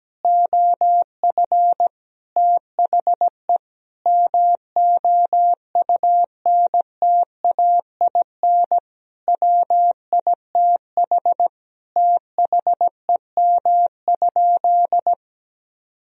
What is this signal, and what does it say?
Morse code, heard as a keyed tone, about 17 wpm, 700 Hz. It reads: OF THE MOUNTAIN WITH THEM?
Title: Sentences from Top 1000 Words - Review Part 3 - 17wpm